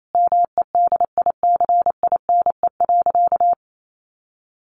Morse code, heard as a keyed tone, about 28 wpm, 700 Hz. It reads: MEDICINE.
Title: Sentences from Top 900 Words 28wpm